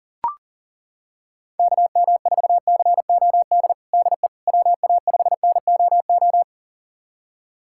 Morse code, heard as a keyed tone, about 41 wpm, 690 Hz. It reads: KM4COD DE WA5NOO
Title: USA Callsign DE Callsign - 6 Characters - 40wpm